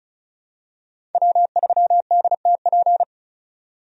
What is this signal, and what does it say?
Morse code, heard as a keyed tone, about 34 wpm, 700 Hz. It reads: W3DTP
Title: USA Call Signs - 5 Characters - 35wpm